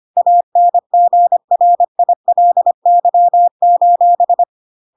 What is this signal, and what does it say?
Morse code, heard as a keyed tone, about 25 wpm, 700 Hz. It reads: ANGRILY:
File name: Pinocchio / Pinocchio ch3 25 25 ik1hge